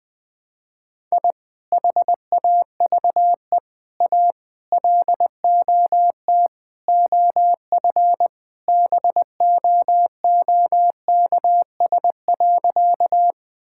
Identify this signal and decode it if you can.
Morse code, heard as a keyed tone, about 20 wpm, 700 Hz. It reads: I HAVE A LOT OF BOOKS.
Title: Sentences with 5-Word Phrases - 20wpm